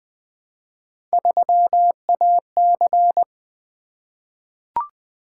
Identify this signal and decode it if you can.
Morse code, heard as a keyed tone, about 20 wpm, 700 Hz. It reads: 3AC
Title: Three Letter-Number Combinations - 20wpm